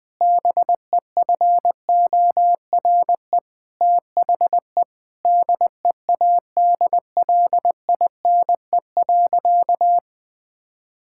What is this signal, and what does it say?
Morse code, heard as a keyed tone, about 20 wpm, 700 Hz. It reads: BEFORE THE DEADLINE.
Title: Sentences with 8-Word Phrases - Spoken Beginning - 20wpm